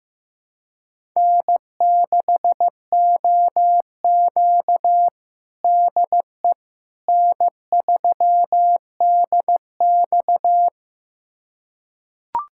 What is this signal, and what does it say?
Morse code, heard as a keyed tone, about 15 wpm, 700 Hz. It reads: N6OQ DE N3DX
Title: USA Callsign DE Callsign - 4 Characters - 15wpm